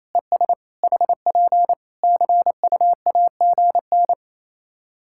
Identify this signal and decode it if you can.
Morse code, heard as a keyed tone, about 28 wpm, 700 Hz. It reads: ES HP CUAGN
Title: Practice QSO - Exchange 4 - Ending - 28wpm